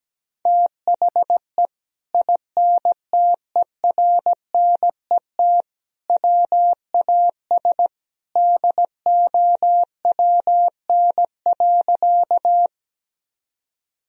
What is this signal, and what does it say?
Morse code, heard as a keyed tone, about 17 wpm, 700 Hz. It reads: THE INTERNET WAS DOWN.
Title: Sentences from Top 1000 Words - Review Part 4 - 17wpm